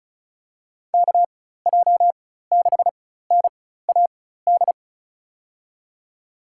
Morse code, heard as a keyed tone, about 34 wpm, spaced as for 16 wpm, 700 Hz. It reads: KJ6NAD
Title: USA Call Signs - 6 Characters - 4x Character Spacing - 35wpm